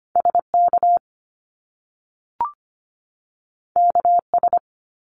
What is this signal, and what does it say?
Morse code, heard as a keyed tone, about 25 wpm, 700 Hz. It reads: SK KS
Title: Top Two Letter Combinations in Words 25wpm